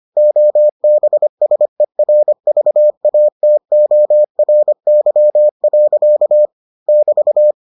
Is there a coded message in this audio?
Yes: Morse code from 0.2 to 7.5 s, heard as a keyed tone, about 25 wpm, 600 Hz. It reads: OBSERVATORY. <BT>